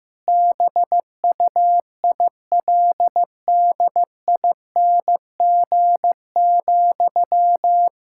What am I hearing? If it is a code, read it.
Morse code, heard as a keyed tone, about 15 wpm, 700 Hz. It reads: BUILDING,
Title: Sentences from Top 1000 Words - Review Part 5 - 15wpm